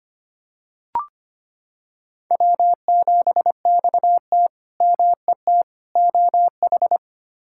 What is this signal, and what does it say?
Morse code, heard as a keyed tone, about 25 wpm, 700 Hz. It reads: W7XT MET OH
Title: K1USN SST QSO Practice - 25wpm